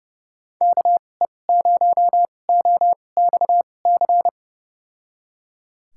Morse code, heard as a keyed tone, about 31 wpm, spaced as for 22 wpm, 700 Hz. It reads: KE0OXC
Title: USA Call Signs - 6 Characters - 2x Character Spacing - 30wpm